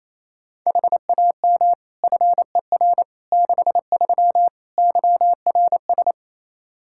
Morse code, heard as a keyed tone, about 28 wpm, 700 Hz. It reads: HAM FER 63 YRS
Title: Practice QSO - Exchange 3 - More Info - 28wpm